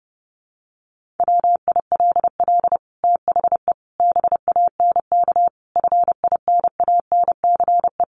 Morse code, heard as a keyed tone, about 30 wpm, 700 Hz. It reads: WILL THE BANK FINANCE
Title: Sentences from Top 1500 Words 30wpm